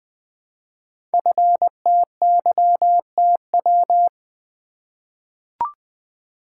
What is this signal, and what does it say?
Morse code, heard as a keyed tone, about 20 wpm, 700 Hz. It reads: FTYTW
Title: Code Groups - 5 Letters - 20wpm